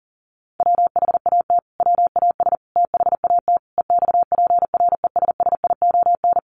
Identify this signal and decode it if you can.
Morse code, heard as a keyed tone, about 40 wpm, 700 Hz. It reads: WHAT WAS THAT EXPRESSION
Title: Sentences from Top 1200 Words 40wpm